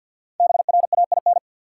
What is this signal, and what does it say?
Morse code, heard as a keyed tone, about 46 wpm, 700 Hz. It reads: BRAIN
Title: Sentences from Top 1000 Words - Review Part 6 - New Word Drill - 50wpm